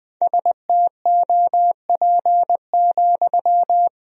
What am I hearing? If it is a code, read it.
Morse code, heard as a keyed tone, about 20 wpm, 700 Hz. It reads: STOP,
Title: 4x Word-Spacing - Sentences from Top 300 Words - 20wpm